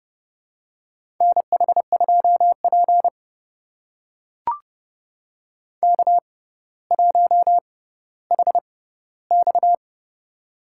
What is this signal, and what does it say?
Morse code, heard as a keyed tone, about 30 wpm, spaced as for 15 wpm, 700 Hz. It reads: NH2P K1HX
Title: USA Call Signs - 4 Characters - 6x Character Spacing - 30wpm